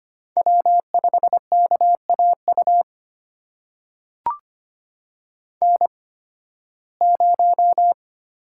Morse code, heard as a keyed tone, about 25 wpm, spaced as for 16 wpm, 700 Hz. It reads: W5KAU N0
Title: USA Call Signs - 5 Characters - 8x Character Spacing - 25wpm